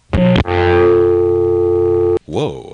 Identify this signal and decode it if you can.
Dial tone (North America), 350 + 440 Hz.